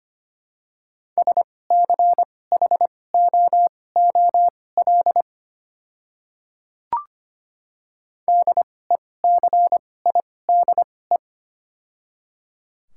Morse code, heard as a keyed tone, about 25 wpm, spaced as for 18 wpm, 700 Hz. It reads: SCHOOL DECIDE
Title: 2x Character Spacing - Top 500 Words - 25wpm